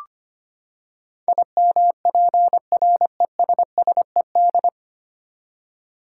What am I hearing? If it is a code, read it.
Morse code, heard as a keyed tone, about 25 wpm, 700 Hz. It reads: IMPRESSED
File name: Nine Letter Words - 25wpm